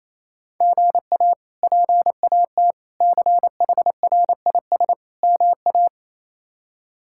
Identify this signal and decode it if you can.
Morse code, heard as a keyed tone, about 28 wpm, 700 Hz. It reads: GA PAT CHRIS MA
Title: K1USN SST QSO Practice - 28wpm